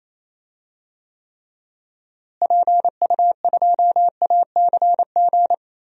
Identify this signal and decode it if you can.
Morse code, heard as a keyed tone, about 28 wpm, 700 Hz. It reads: PU2ACG
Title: DXCC-100 Call Signs - 6 Characters - Repeated 2x - 28wpm